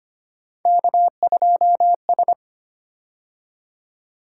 Morse code, heard as a keyed tone, about 25 wpm, 700 Hz. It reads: K2S
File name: Three Letter-Number Combinations - 25wpm